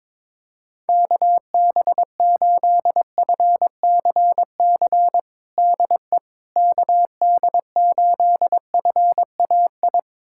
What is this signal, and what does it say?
Morse code, heard as a keyed tone, about 22 wpm, 700 Hz. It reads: KB8FCC DE KD8FAI
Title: USA Callsign DE Callsign 22wpm